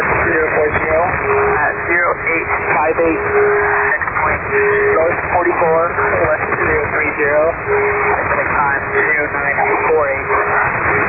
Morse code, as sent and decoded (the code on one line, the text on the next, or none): none